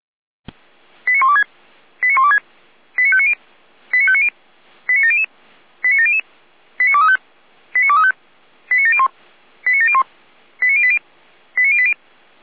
Five-tone selective call (ZVEI), heard as quick runs of five tones, tones of about 70 ms. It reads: 89126 89126 89590 89590 89700 89700 89235 89235 89701 89701 89080 89080